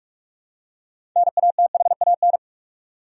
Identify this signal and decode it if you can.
Morse code, heard as a keyed tone, about 46 wpm, 700 Hz. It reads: NATHAN
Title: USA Common Names - 45wpm